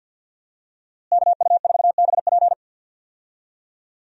Morse code, heard as a keyed tone, about 51 wpm, 700 Hz. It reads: KU4BP
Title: USA Call Signs - 5 Characters - Repeated 2x - 50wpm